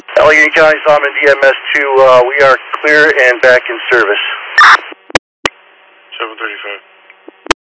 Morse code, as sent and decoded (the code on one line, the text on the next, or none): none